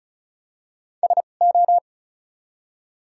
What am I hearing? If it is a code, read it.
Morse code, heard as a keyed tone, about 34 wpm, spaced as for 25 wpm, 700 Hz. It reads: SO
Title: 2x Character Spacing - Top 100 Words - 35wpm